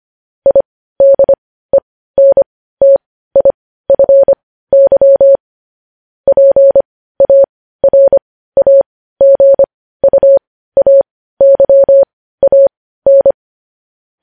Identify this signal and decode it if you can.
Morse code, heard as a keyed tone, about 25 wpm, spaced as for 15 wpm, 550 Hz. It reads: IDENTIFY PARAGUAYAN
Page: news-headlines-in-morse-code-at-15-wpm